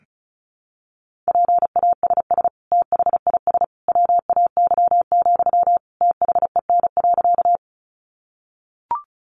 Morse code, heard as a keyed tone, about 34 wpm, 700 Hz. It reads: PASS THIS WAY, THEN.
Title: Sentences from Top 400 Words 35wpm